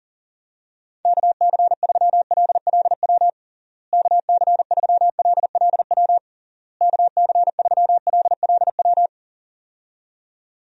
Morse code, heard as a keyed tone, about 40 wpm, 700 Hz. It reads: KC3LLW KC3LLW KC3LLW
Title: USA Call Signs - 6 Characters - Repeated 3x - 40wpm